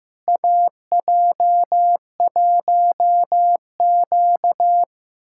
Morse code, heard as a keyed tone, about 15 wpm, 700 Hz. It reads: AJ1Q
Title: USA Callsign DE Callsign - 4 Characters - 15wpm